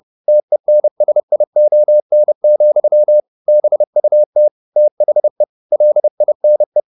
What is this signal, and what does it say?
Morse code, heard as a keyed tone, about 30 wpm, 600 Hz. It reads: TENSION, BUT THE LINE
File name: the old man and the sea / the old man and the sea 1 30wpm